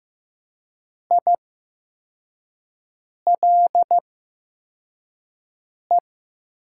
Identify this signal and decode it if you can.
Morse code, heard as a keyed tone, about 15 wpm, spaced as for 4 wpm, 700 Hz. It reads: ILE
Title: 8x Character Spacing - Top 300 Words - 15wpm